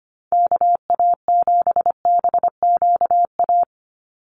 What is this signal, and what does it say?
Morse code, heard as a keyed tone, about 25 wpm, 690 Hz. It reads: KA7BQA
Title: USA Call Signs 25wpm